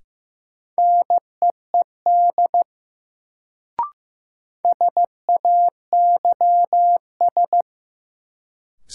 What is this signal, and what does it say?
Morse code, heard as a keyed tone, about 15 wpm, 700 Hz. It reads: NEED SAYS
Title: Sets of 2 Words - New Word Drill - 15wpm